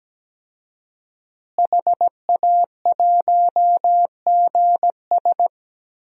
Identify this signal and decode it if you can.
Morse code, heard as a keyed tone, about 17 wpm, 700 Hz. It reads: HA1GS
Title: DXCC-100 Call Signs - 5 Characters - Repeated 2x - 17wpm